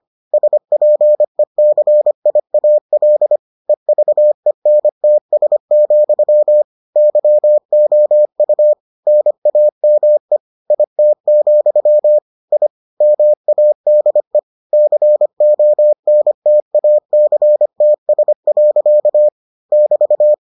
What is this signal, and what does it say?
Morse code, heard as a keyed tone, about 25 wpm, 600 Hz. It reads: SPECIAL EVENTS, YOU NAME IT, I MADE CONTACTS. <BT>